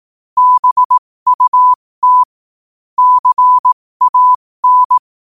Morse code, heard as a keyed tone, about 18 wpm, spaced as for 15 wpm, 1000 Hz. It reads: BUT CAN
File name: qotdfiles / 15wpm